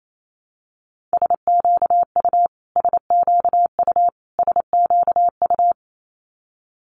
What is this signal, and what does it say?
Morse code, heard as a keyed tone, about 28 wpm, 700 Hz. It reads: SQU SQU SQU
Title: Multi-letter Phonemes Repeated 3x 28wpm